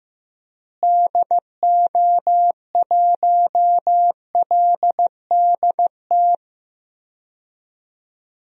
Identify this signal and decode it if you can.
Morse code, heard as a keyed tone, about 16 wpm, 700 Hz. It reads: DO1LDT
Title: DXCC-100 Call Signs - 6 Characters - Repeated 2x - 15wpm